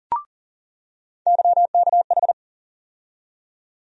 Morse code, heard as a keyed tone, about 40 wpm, 700 Hz. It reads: YKH
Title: USA Call Sign Postfixes - 3 Letters 40wpm